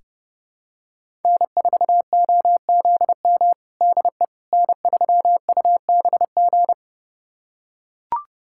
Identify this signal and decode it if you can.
Morse code, heard as a keyed tone, about 32 wpm, 700 Hz. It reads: N4OZM DE N3UBG
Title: USA Callsign DE Callsign - 5 Characters - 30wpm